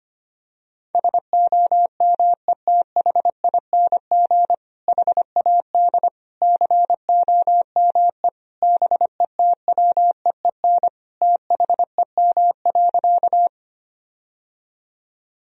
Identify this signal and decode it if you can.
Morse code, heard as a keyed tone, about 25 wpm, 700 Hz. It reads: SOMETHING HAD COME BETWEEN THEM.
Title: Sentences from Top 1000 Words - Review Part 1 - 25wpm